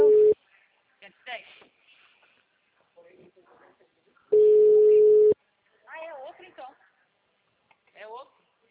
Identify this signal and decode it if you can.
Ringback tone (Europe, 425 Hz), with speech.